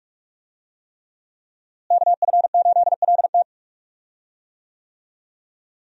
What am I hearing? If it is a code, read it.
Morse code, heard as a keyed tone, about 46 wpm, 700 Hz. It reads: KF8LT
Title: USA Call Signs - 5 Characters - Repeated 3x - 45wpm